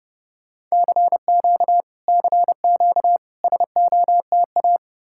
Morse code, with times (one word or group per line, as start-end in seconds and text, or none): 0.7-1.8 CQ
2.1-3.2 CQ
3.4-4.8 SOTA